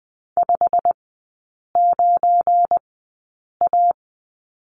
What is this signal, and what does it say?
Morse code, heard as a keyed tone, about 20 wpm, spaced as for 8 wpm, 690 Hz. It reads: 59A
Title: 172 Lesson 28e - Number 9 - Three Characters 20wpm